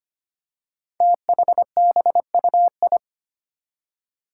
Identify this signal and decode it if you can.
Morse code, heard as a keyed tone, about 25 wpm, 700 Hz. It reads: THBUI
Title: Five Challenging Letter-Number Combinations 25wpm